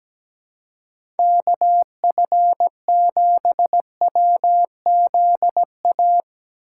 Morse code, heard as a keyed tone, about 17 wpm, 700 Hz. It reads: KF7WZA
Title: USA Call Signs - 4x Character Spacing - 17wpm